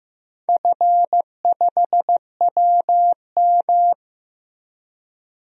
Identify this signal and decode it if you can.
Morse code, heard as a keyed tone, about 15 wpm, 700 Hz. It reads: F5WM
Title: DXCC-100 Call Signs - 4 Characters - Repeated 3x - 15wpm